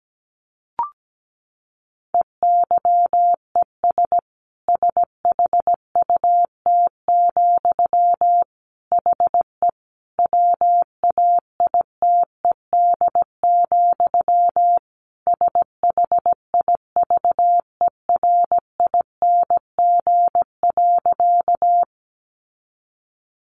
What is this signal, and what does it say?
Morse code, heard as a keyed tone, about 17 wpm, 700 Hz. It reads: EYES SHUT, HE WAITED, SHIVERING.
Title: A Sound Of Thunder 17wpm